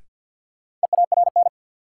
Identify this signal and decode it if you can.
Morse code, heard as a keyed tone, about 51 wpm, 700 Hz. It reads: EARN
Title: Sentences from Top 1500 Words - New Word Drill - 50wpm